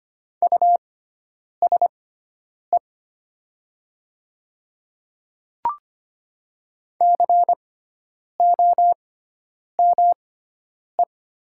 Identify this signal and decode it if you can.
Morse code, heard as a keyed tone, about 25 wpm, spaced as for 9 wpm, 700 Hz. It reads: USE COME
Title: 6x Character Spacing - Top 100 Words - 25wpm